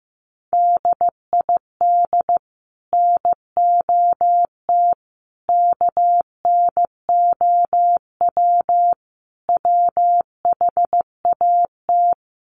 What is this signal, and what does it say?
Morse code, heard as a keyed tone, about 15 wpm, 700 Hz. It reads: DID NOT KNOW WHAT
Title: Sets of 5 Words 15wpm